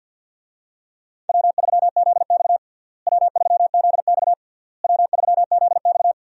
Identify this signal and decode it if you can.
Morse code, heard as a keyed tone, about 49 wpm, 700 Hz. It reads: W3ZX W3ZX W3ZX
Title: USA Call Signs - 4 Characters - Repeated 3x - 50wpm